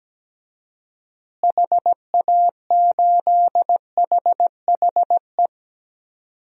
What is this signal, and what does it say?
Morse code, heard as a keyed tone, about 17 wpm, 700 Hz. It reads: HA8HHE
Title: DXCC-100 Call Signs - 6 Characters - Repeated 2x - 17wpm